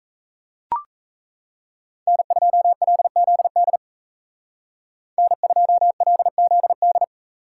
Morse code, heard as a keyed tone, about 37 wpm, 700 Hz. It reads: N2LZD N2LZD